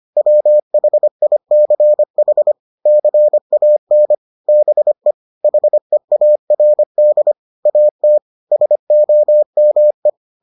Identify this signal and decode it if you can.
Morse code, heard as a keyed tone, about 25 wpm, 600 Hz. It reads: WHICH CAN BE HEARD AT SOME